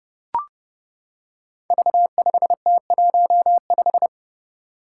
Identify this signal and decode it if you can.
Morse code, heard as a keyed tone, about 29 wpm, 700 Hz. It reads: V5T15
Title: Five Challenging Letter-Number Combinations 30wpm